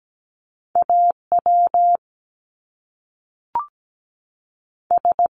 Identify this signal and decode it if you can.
Morse code, heard as a keyed tone, about 17 wpm, 700 Hz. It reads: A W S
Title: Top Two Letter Combinations in Words 17wpm